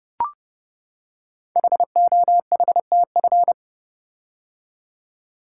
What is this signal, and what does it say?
Morse code, heard as a keyed tone, about 30 wpm, 700 Hz. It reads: HOHTF
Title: Code Groups - 5 Letters - 30wpm